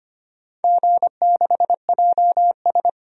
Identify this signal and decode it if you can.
Morse code, heard as a keyed tone, about 25 wpm, 700 Hz. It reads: G6JS
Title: DXCC-100 Call Signs - 4 Characters - 25wpm